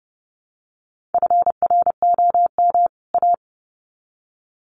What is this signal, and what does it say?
Morse code, heard as a keyed tone, about 30 wpm, 700 Hz. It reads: FROM A
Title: Sets of 2 Words 30wpm